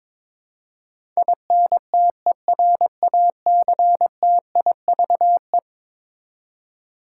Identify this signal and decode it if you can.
Morse code, heard as a keyed tone, about 22 wpm, 700 Hz. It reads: INTERACTIVE
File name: practice / 22wpm / Eleven Letter Words - 22wpm